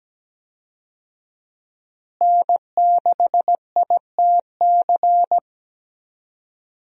Morse code, heard as a keyed tone, about 17 wpm, 700 Hz. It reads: N6ITC
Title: USA Call Signs - 5 Characters - Repeated 2x - 17wpm